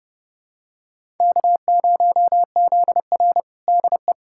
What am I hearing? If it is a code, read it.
Morse code, heard as a keyed tone, about 29 wpm, 700 Hz. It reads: K0ZR DE
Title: Practice QSO - Exchange 4 - Ending - 30wpm